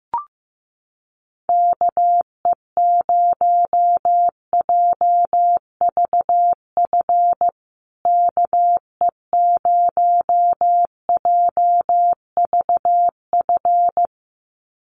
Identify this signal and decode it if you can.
Morse code, heard as a keyed tone, about 15 wpm, 700 Hz. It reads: KE0JVF KE0JVF
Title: USA Call Signs Repeated 2x 15wpm